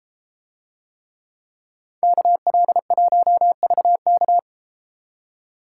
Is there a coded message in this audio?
Yes: Morse code from 2.0 to 4.4 s, heard as a keyed tone, about 30 wpm, 700 Hz. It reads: KL1VK